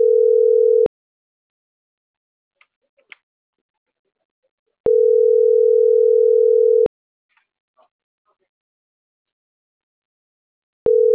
Ringback tone (North America), 440 + 480 Hz.